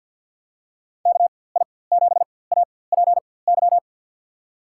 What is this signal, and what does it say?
Morse code, heard as a keyed tone, about 53 wpm, spaced as for 24 wpm, 700 Hz. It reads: KI7APY
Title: USA Call Signs - 4x Character Spacing - 50wpm